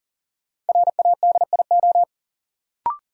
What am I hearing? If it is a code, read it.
Morse code, heard as a keyed tone, about 40 wpm, 700 Hz. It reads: RADIO
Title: Sentences from Top 1000 Words - Review Part 3 - New Word Drill - 40wpm